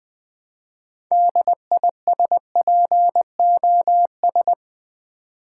Morse code, heard as a keyed tone, about 20 wpm, 700 Hz. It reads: DISPOS
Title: Build A Word - 2 Letters At A Time - 8-Letter Words - 20wpm